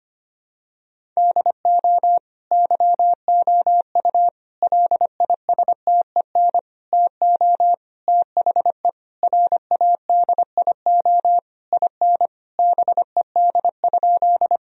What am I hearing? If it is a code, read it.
Morse code, heard as a keyed tone, about 25 wpm, 700 Hz. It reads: DO YOU LISTEN TO THE RADIO IN BED?
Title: Sentences from Top 1000 Words - Review Part 3 - 25wpm